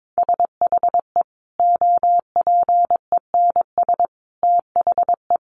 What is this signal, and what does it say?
Morse code, heard as a keyed tone, about 22 wpm, 700 Hz. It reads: SHE OPENS THE
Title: Additional Easy Phrases 22wpm